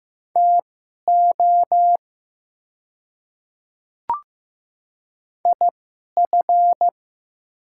Morse code, heard as a keyed tone, about 15 wpm, spaced as for 11 wpm, 700 Hz. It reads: TO IF
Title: 2x Character Spacing - Top 100 Words - 15wpm